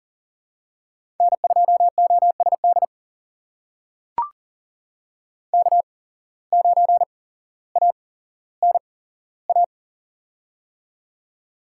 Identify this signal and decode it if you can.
Morse code, heard as a keyed tone, about 39 wpm, spaced as for 17 wpm, 700 Hz. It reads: N2OSD K9ANA
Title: USA Call Signs - 5 Characters - 8x Character Spacing - 40wpm